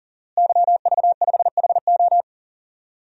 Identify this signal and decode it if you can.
Morse code, heard as a keyed tone, about 40 wpm, 700 Hz. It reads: YV5HO
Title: DXCC-100 Call Signs - 5 Characters - Repeated 2x - 40wpm